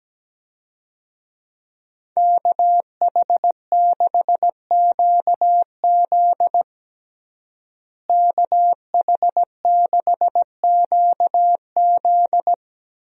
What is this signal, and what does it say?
Morse code, heard as a keyed tone, about 17 wpm, 700 Hz. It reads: KH6QZ KH6QZ
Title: USA Call Signs - 5 Characters - Repeated 2x - 17wpm